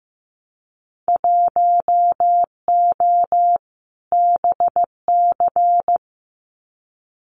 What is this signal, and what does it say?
Morse code, heard as a keyed tone, about 15 wpm, 700 Hz. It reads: 1O BC
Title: Winter Field Day Exchanges Encore 15wpm